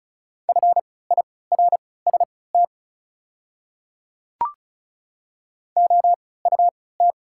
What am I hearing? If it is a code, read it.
Morse code, heard as a keyed tone, about 36 wpm, spaced as for 20 wpm, 700 Hz. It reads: FIRST OUT
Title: 3x Character Spacing - Top 200 Words - 35wpm